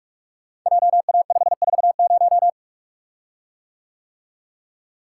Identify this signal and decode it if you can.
Morse code, heard as a keyed tone, about 42 wpm, 700 Hz. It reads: JA540
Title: Code Groups - 5 Letter:Numbers - 45wpm